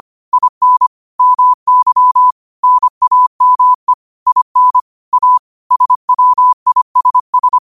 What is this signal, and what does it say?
Morse code, heard as a keyed tone, about 25 wpm, 1000 Hz. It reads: IN MY NAME IN A SWISS
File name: qotdfiles / 25wpm